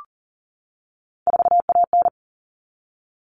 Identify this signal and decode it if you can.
Morse code, heard as a keyed tone, about 40 wpm, 700 Hz. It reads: 4AN
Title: Common Cut Numbers - 3-Digit 40wpm